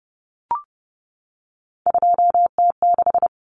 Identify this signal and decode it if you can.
Morse code, heard as a keyed tone, about 30 wpm, 700 Hz. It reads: 2T6
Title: Common Cut Numbers - 3-Digit 30wpm